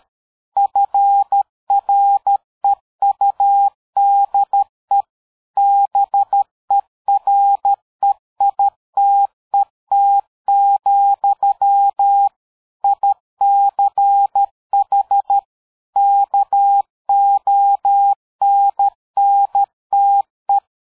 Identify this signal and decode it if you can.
Morse code, heard as a keyed tone, about 13 wpm, 800 Hz. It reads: FREUDE BEREITET, ICH KONNTE
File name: dk5ke / PersWorte 60